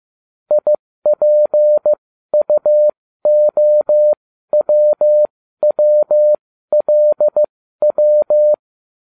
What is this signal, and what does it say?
Morse code, heard as a keyed tone, about 15 wpm, spaced as for 12 wpm, 600 Hz. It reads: IPUOWWLW